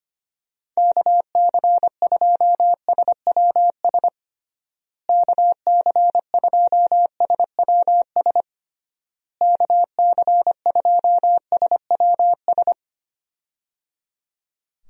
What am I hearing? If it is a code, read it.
Morse code, heard as a keyed tone, about 25 wpm, 700 Hz. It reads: KC2SWS KC2SWS KC2SWS
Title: USA Call Signs - 6 Characters - Repeated 3x - 25wpm